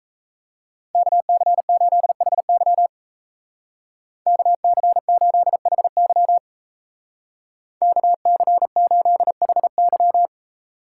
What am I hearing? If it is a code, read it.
Morse code, heard as a keyed tone, about 34 wpm, 700 Hz. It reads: KC8HY KC8HY KC8HY
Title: Speed Racing - USA 2x2 Call Signs 28wpm